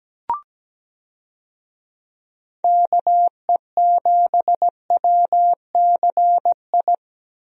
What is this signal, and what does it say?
Morse code, heard as a keyed tone, about 17 wpm, 700 Hz. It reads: KE7WCI
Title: USA Call Signs - 6 Characters - Repeated 3x - 17wpm